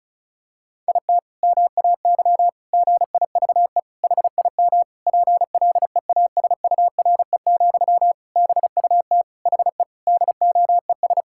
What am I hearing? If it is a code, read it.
Morse code, heard as a keyed tone, about 36 wpm, 700 Hz. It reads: IT MAY GIVE HIM PLEASURE, BUT HE DOES
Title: Sentences from Top 1600 Words 35wpm